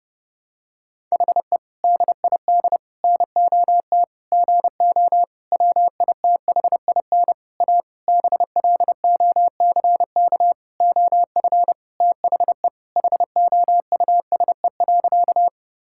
Morse code, heard as a keyed tone, about 32 wpm, 700 Hz. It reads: HE DID NOT GO WITHIN A BLOCK OF THE HOUSE.
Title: Sentences from Top 1600 Words 30wpm